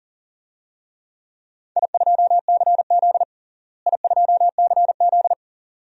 Morse code, heard as a keyed tone, about 40 wpm, 700 Hz. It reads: I2CZ I2CZ
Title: DXCC-100 Call Signs - 4 Characters - Repeated 2x - 40wpm